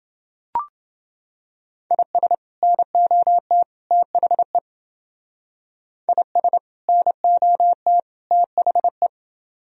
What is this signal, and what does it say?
Morse code, heard as a keyed tone, about 28 wpm, 700 Hz. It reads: IS NOT THE IS NOT THE